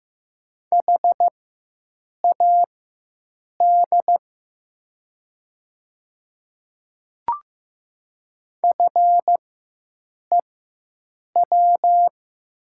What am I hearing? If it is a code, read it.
Morse code, heard as a keyed tone, about 15 wpm, spaced as for 7 wpm, 700 Hz. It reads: HAD FEW
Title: 4x Character Spacing - Top 300 Words - 15wpm